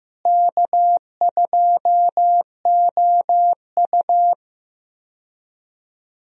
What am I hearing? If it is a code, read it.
Morse code, heard as a keyed tone, about 15 wpm, 700 Hz. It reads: K2OU
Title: USA Call Signs - 4 Characters - Repeated 3x - 15wpm